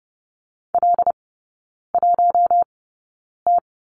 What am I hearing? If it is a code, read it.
Morse code, heard as a keyed tone, about 30 wpm, spaced as for 9 wpm, 690 Hz. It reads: L1T
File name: Lesson 17e - Letter u - Three Characters 30wpm